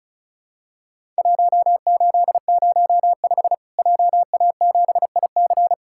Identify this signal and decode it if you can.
Morse code, heard as a keyed tone, about 34 wpm, 700 Hz. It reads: 1805 JA7IC
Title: Worked All Europe DX Contest - QTC Exchange - 35wpm